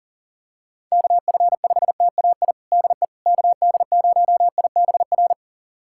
Morse code, heard as a keyed tone, about 37 wpm, 700 Hz. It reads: KF5TAI DE KD0IBR
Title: USA Callsign DE Callsign 40wpm